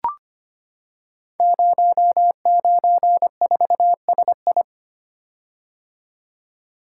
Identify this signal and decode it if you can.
Morse code, heard as a keyed tone, about 26 wpm, 700 Hz. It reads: 094SI
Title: Code Groups - 5 Letter:Numbers - 25wpm